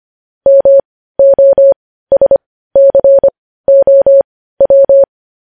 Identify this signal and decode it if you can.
Morse code, heard as a keyed tone, about 25 wpm, spaced as for 15 wpm, 550 Hz. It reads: MOSCOW
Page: news-headlines-in-morse-code-at-15-wpm